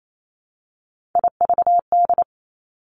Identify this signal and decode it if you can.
Morse code, heard as a keyed tone, about 28 wpm, 700 Hz. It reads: IVD
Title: Three Challenging Letter-Number Combinations 28wpm